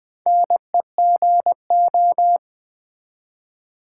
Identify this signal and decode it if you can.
Morse code, heard as a keyed tone, about 20 wpm, 700 Hz. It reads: NEGO
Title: Build A Word - 4 Letters At A Time - 11-Letter Words - 20wpm